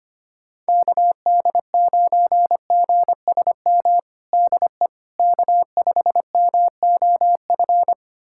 Morse code, heard as a keyed tone, about 25 wpm, 700 Hz. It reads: KD9GSM DE K5MOF
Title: USA Callsign DE Callsign 25wpm